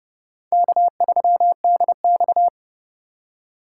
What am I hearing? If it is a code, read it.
Morse code, heard as a keyed tone, about 29 wpm, 700 Hz. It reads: K3DX